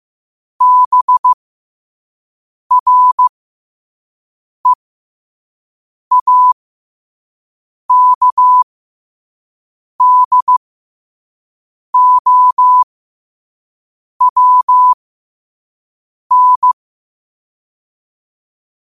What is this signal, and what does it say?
Morse code, heard as a keyed tone, about 15 wpm, spaced as for 5 wpm, 1000 Hz. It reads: BREAKDOWN
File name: qotdfiles / 5wpm